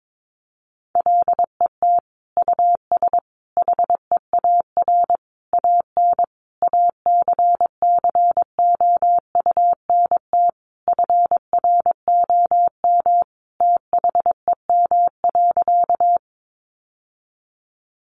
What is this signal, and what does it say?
Morse code, heard as a keyed tone, about 22 wpm, 700 Hz. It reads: LET US HEAR AN ACCOUNT FROM THEM.
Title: Sentences from Top 900 Words 22wpm